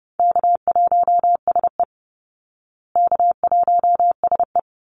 Morse code, heard as a keyed tone, about 30 wpm, 700 Hz. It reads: K1SE K1SE
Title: Easy Call Signs Repeated 2x - 30wpm